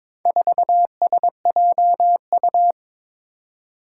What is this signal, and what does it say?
Morse code, heard as a keyed tone, about 22 wpm, 700 Hz. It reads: 4SJU
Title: Four Challenging Letter-Number Combinations 22wpm